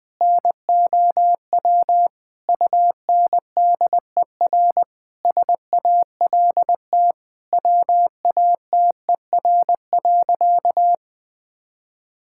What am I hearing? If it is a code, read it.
Morse code, heard as a keyed tone, about 19 wpm, 700 Hz. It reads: NOW UNDER SALT WATER.
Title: Sentences from Top 1000 Words - Review Part 5 - 20wpm